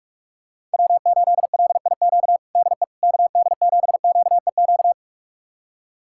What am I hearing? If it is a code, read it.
Morse code, heard as a keyed tone, about 44 wpm, 700 Hz. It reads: W8LIQ DE KD7QEQ
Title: USA Callsign DE Callsign 45wpm